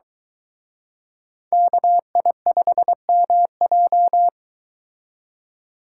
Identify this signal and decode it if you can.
Morse code, heard as a keyed tone, about 23 wpm, 700 Hz. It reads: KI5MJ